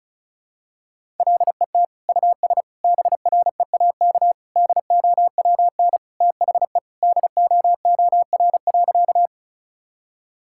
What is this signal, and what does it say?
Morse code, heard as a keyed tone, about 34 wpm, 700 Hz. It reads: LET US BREAK DOWN THE DOOR.
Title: Sentences from Top 1000 Words - Review Part 6 - 35wpm